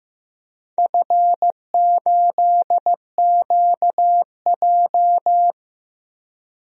Morse code, heard as a keyed tone, about 15 wpm, 700 Hz. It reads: F8QJ
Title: DXCC-100 Call Signs - 4 Characters - 15wpm